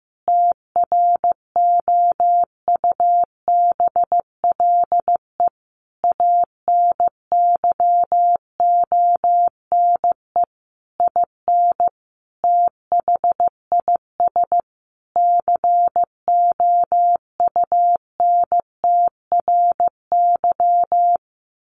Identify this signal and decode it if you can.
Morse code, heard as a keyed tone, about 15 wpm, 700 Hz. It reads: TROUBLE ANYONE IN THIS COUNTRY
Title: Sentences from Top 1200 Words 15wpm